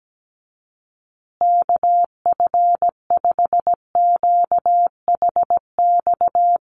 Morse code, heard as a keyed tone, about 17 wpm, 700 Hz. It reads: KF5QHX
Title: USA Call Signs Repeated 2x 17wpm